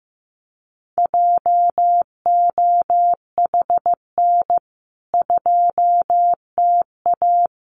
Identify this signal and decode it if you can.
Morse code, heard as a keyed tone, about 15 wpm, 700 Hz. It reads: JOHN 2TA
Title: CWT Exchange 15wpm